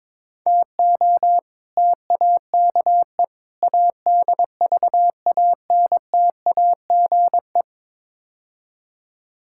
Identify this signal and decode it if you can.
Morse code, heard as a keyed tone, about 22 wpm, 700 Hz. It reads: TO TAKE ADVANTAGE
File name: practice / 22wpm / Build A Phrase - 5-Word Phrases - 22wpm